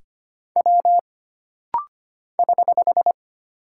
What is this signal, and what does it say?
Morse code, heard as a keyed tone, about 25 wpm, 700 Hz. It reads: W <HH>
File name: Single Character (Letter:Number:Punctuation:Prosign) - Rapid-Fire - 25wpm